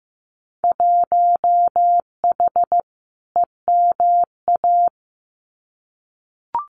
Morse code, heard as a keyed tone, about 15 wpm, 700 Hz. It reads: 1H EMA
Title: Winter Field Day Exchanges 15wpm